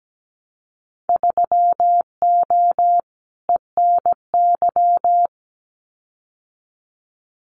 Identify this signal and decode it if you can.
Morse code, heard as a keyed tone, about 17 wpm, 700 Hz. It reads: 3O ENY
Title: Winter Field Day Exchanges Encore 17wpm